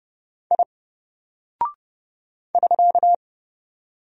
Morse code, heard as a keyed tone, about 30 wpm, 700 Hz. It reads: I <SK>
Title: Single Character (Letter:Number:Punctuation:Prosign) 30wpm